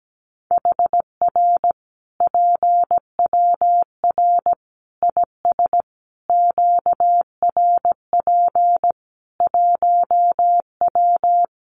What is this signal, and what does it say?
Morse code, heard as a keyed tone, about 17 wpm, 700 Hz. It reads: HR PWR IS QRP 1W
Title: Common QSO Phrases 17wpm